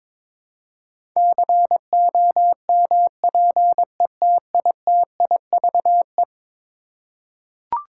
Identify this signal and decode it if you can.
Morse code, heard as a keyed tone, about 22 wpm, 700 Hz. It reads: COMPETITIVE
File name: practice / 22wpm / Eleven Letter Words - 22wpm